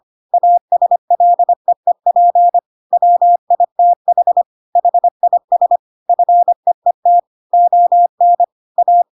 Morse code, heard as a keyed tone, about 25 wpm, 700 Hz. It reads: ASLEEP WITH HIS FEET ON A
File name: Pinocchio / Pinocchio ch6 25 25 ik1hge